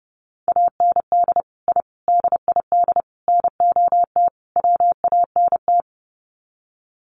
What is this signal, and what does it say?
Morse code, heard as a keyed tone, about 30 wpm, 700 Hz. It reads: AND I DID NOT WANT
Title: Sets of 5 Words 30wpm